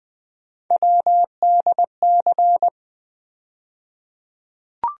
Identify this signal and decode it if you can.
Morse code, heard as a keyed tone, about 21 wpm, 700 Hz. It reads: WDC
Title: USA Call Sign Postfixes - 3 Letters 20wpm